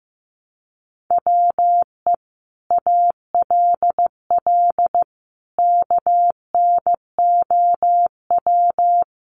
Morse code, heard as a keyed tone, about 15 wpm, 700 Hz. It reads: WE ALL KNOW
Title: Sentences from Top 1200 Words 15wpm